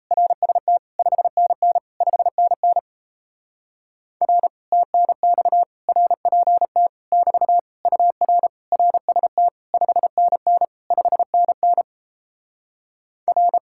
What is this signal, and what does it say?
Morse code, heard as a keyed tone, about 38 wpm, 700 Hz. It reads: RST 5NN 5NN R TNX RPT <BT> UR RST 5NN 5NN R